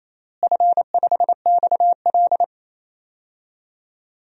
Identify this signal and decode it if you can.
Morse code, heard as a keyed tone, about 28 wpm, 700 Hz. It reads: F5XL
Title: DXCC-100 Call Signs - 4 Characters - Repeated 3x - 28wpm